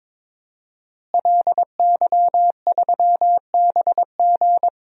Morse code, heard as a keyed tone, about 22 wpm, 700 Hz. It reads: LY3BG
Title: DXCC-100 Call Signs - 5 Characters - Repeated 2x - 22wpm